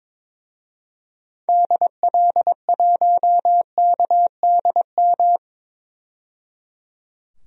Morse code, heard as a keyed tone, about 22 wpm, 700 Hz. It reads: DL1KDM
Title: DXCC-100 Call Signs - 6 Characters - 22wpm